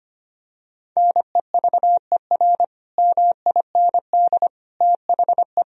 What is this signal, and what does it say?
Morse code, heard as a keyed tone, about 25 wpm, 700 Hz. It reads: NEVER MIND THE
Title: Sentences from Top 1000 Words - Review Part 1 - 25wpm